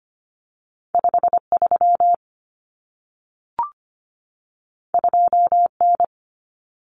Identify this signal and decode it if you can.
Morse code, heard as a keyed tone, about 25 wpm, 700 Hz. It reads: 53 2N
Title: Common Cut Numbers - 2-Digit 25wpm